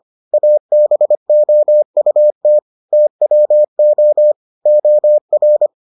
Morse code, heard as a keyed tone, about 25 wpm, 600 Hz. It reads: ABOUT TWO OR